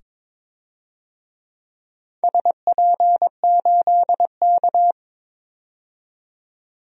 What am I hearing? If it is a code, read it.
Morse code, heard as a keyed tone, about 22 wpm, 700 Hz. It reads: SP8K
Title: DXCC-100 Call Signs - 4 Characters - 22wpm